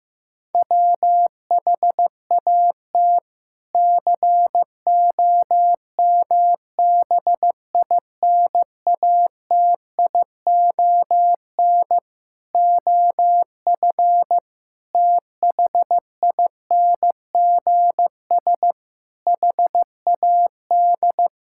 Morse code, heard as a keyed tone, about 15 wpm, 700 Hz. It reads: WHAT COMBINATION OF THINGS HAD
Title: Sentences from Top 1000 Words - Review Part 6 - 15wpm